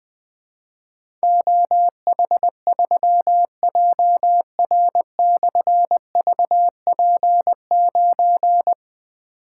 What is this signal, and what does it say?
Morse code, heard as a keyed tone, about 20 wpm, 700 Hz. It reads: OH3JR/VP9
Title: Difficult Call Signs 20wpm